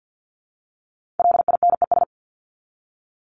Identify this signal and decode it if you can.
Morse code, heard as a keyed tone, about 50 wpm, 700 Hz. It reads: LINES
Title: ES:S Suffix - Top 100 Words with Base Words 50wpm